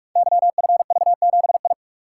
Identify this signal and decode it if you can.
Morse code, heard as a keyed tone, about 44 wpm, 700 Hz. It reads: YFV7I
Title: Code Groups - 5 Letter:Numbers - 45wpm